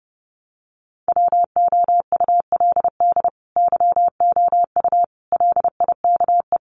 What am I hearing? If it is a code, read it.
Morse code, heard as a keyed tone, about 30 wpm, 700 Hz. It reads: WOULD YOU LIKE
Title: Sentences from Top 100 Words 30wpm